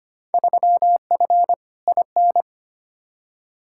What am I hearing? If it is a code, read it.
Morse code, heard as a keyed tone, about 25 wpm, 700 Hz. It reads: 3F IN
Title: Field Day Exchanges - 25wpm